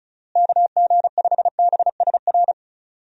Morse code, heard as a keyed tone, about 34 wpm, 700 Hz. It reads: KG5BSR
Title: USA Call Signs - 6 Characters - Repeated 3x - 35wpm